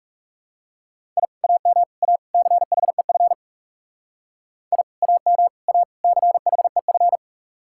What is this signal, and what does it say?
Morse code, heard as a keyed tone, about 38 wpm, 700 Hz. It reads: I AM A CHEF I AM A CHEF